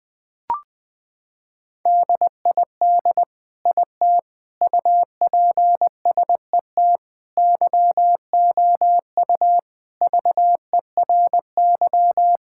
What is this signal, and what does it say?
Morse code, heard as a keyed tone, about 20 wpm, 700 Hz. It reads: DID IT UPSET YOU VERY
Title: Sentences from Top 1000 Words - Review Part 5 - 20wpm